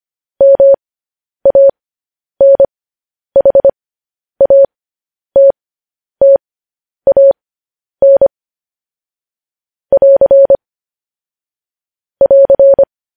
Morse code, heard as a keyed tone, about 25 wpm, spaced as for 10 wpm, 550 Hz. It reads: MANHATTAN <AR> <AR>